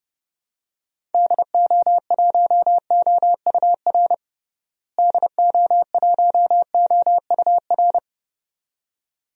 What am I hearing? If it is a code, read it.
Morse code, heard as a keyed tone, about 30 wpm, 700 Hz. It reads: DO1OUR DO1OUR
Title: DXCC-100 Call Signs - 6 Characters - Repeated 2x - 30wpm